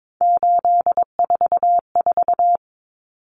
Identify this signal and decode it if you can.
Morse code, heard as a keyed tone, about 22 wpm, 700 Hz. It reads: 844
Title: Common Cut Numbers - 3-Digit 22wpm